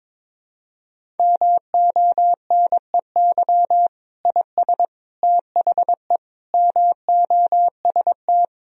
Morse code, heard as a keyed tone, about 22 wpm, 700 Hz. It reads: MONEY IS THE MOST
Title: Sentences from Top 1000 Words - Review Part 1 - 22wpm